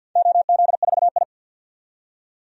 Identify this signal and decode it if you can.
Morse code, heard as a keyed tone, about 51 wpm, 700 Hz. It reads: OZ4I
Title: DXCC-100 Call Signs - 4 Characters - 50wpm